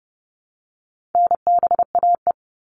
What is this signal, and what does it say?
Morse code, heard as a keyed tone, about 30 wpm, 700 Hz. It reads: NBAE
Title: All Cut Numbers - 4-Digit 30wpm